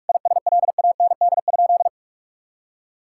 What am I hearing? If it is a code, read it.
Morse code, heard as a keyed tone, about 46 wpm, 700 Hz. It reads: ISLAND?
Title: Sentences from Top 1600 Words 45wpm